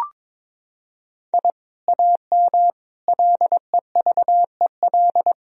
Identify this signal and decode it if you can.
Morse code, heard as a keyed tone, about 22 wpm, 700 Hz. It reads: I AM LEVEL
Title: Sentences from Top 1000 Words - Review Part 4 - 22wpm